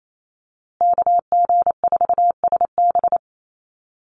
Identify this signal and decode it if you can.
Morse code, heard as a keyed tone, about 28 wpm, 700 Hz. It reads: KG4SB
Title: USA Call Signs Repeated 3x 28wpm